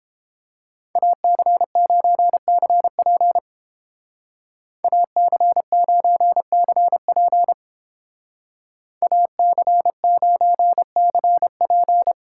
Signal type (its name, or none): Morse code